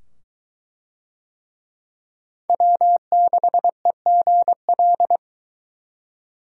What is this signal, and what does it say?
Morse code, heard as a keyed tone, about 21 wpm, 700 Hz. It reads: W6EGL